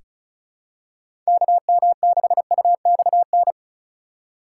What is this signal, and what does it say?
Morse code, heard as a keyed tone, about 36 wpm, 700 Hz. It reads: KM6UXN
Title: USA Call Signs - 6 Characters - 3x Character Spacing - 35wpm